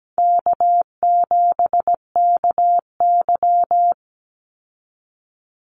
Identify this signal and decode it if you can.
Morse code, heard as a keyed tone, about 17 wpm, 700 Hz. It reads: K7KY
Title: Call Signs Repeated 3x - 17wpm